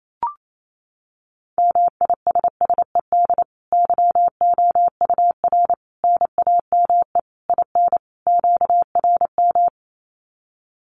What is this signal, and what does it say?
Morse code, heard as a keyed tone, about 28 wpm, 700 Hz. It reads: MISSED YOUR NAME IN QRM
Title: Common QSO Phrases 28wpm